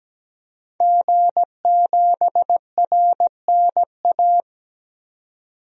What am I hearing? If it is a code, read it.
Morse code, heard as a keyed tone, about 17 wpm, 700 Hz. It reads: G7RNA
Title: DXCC-100 Call Signs - 5 Characters - 17wpm